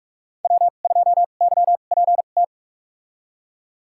Morse code, heard as a keyed tone, about 46 wpm, spaced as for 32 wpm, 700 Hz. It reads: W2YPT
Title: USA Call Signs - 2x Character Spacing - 45wpm